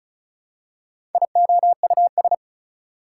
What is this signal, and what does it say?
Morse code, heard as a keyed tone, about 36 wpm, 700 Hz. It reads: IOUS
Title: Prefixes and Suffixes 35wpm